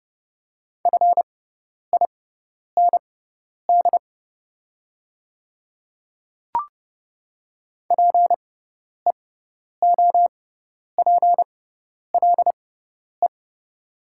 Morse code, heard as a keyed tone, about 31 wpm, spaced as for 10 wpm, 690 Hz. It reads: FIND PEOPLE
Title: 6x Character Spacing - Top 100 Words - 30wpm